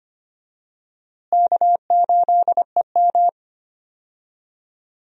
Morse code, heard as a keyed tone, about 25 wpm, 700 Hz. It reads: K8EM
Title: USA Call Signs - 4 Characters - Repeated 3x - 25wpm